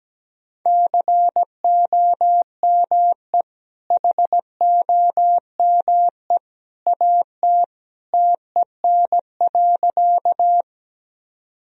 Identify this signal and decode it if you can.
Morse code, heard as a keyed tone, about 17 wpm, 700 Hz. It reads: COME HOME AT TEN.
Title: Sentences from Top 1000 Words - Review Part 2 - 17wpm